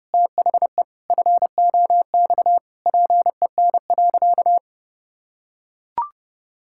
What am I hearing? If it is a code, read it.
Morse code, heard as a keyed tone, about 32 wpm, 700 Hz. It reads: THE FOX PEN.
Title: Pangram Challenge v2 - 30wpm